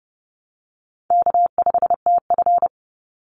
Morse code, heard as a keyed tone, about 30 wpm, 700 Hz. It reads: K5TF
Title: Call Signs Repeated 2x - 30wpm